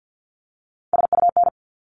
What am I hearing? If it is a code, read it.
Morse code, heard as a keyed tone, about 50 wpm, 700 Hz. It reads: SUN
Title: Top 300 Words 50wpm